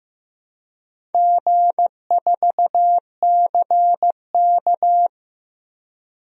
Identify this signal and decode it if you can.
Morse code, heard as a keyed tone, about 15 wpm, 700 Hz. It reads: G4CK
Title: DXCC-100 Call Signs - 4 Characters - Repeated 3x - 15wpm